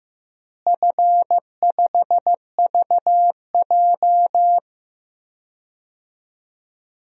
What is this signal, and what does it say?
Morse code, heard as a keyed tone, about 15 wpm, 700 Hz. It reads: F5VJ
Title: DXCC-100 Call Signs - 4 Characters - Repeated 2x - 15wpm